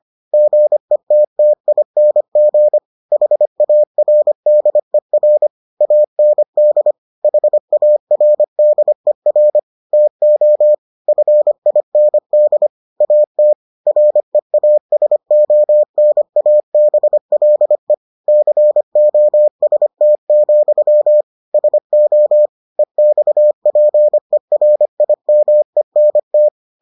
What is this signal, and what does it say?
Morse code, heard as a keyed tone, about 25 wpm, 600 Hz. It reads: GETTING HARDER AND HARDER TO FIND AT REASONABLE COST, SO EXPERIMENT